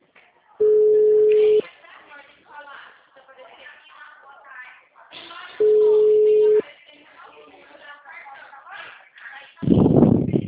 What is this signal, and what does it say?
Ringback tone (Europe, 425 Hz), with speech.
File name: out- -2006- - - .65113